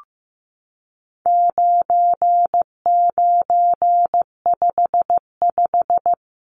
Morse code, heard as a keyed tone, about 15 wpm, 700 Hz. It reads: 9955
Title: Numbers - 4-Digit 15wpm